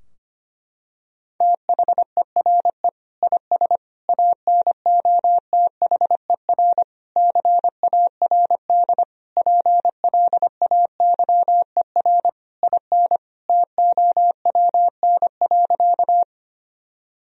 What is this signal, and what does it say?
Morse code, heard as a keyed tone, about 25 wpm, 700 Hz. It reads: THERE IS ANOTHER CARD PLAYER IN TOWN.
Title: Sentences from Top 1000 Words - Review Part 4 - 25wpm